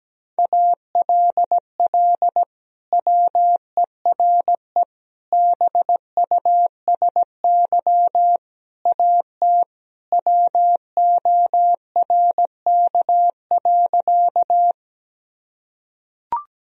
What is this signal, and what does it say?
Morse code, heard as a keyed tone, about 17 wpm, 700 Hz. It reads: ALL WERE BUSY AT WORK.
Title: Sentences from Top 1000 Words - Review Part 3 - 17wpm